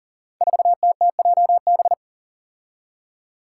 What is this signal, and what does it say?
Morse code, heard as a keyed tone, about 40 wpm, 700 Hz. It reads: 4TTJB
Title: Five Challenging Letter-Number Combinations 40wpm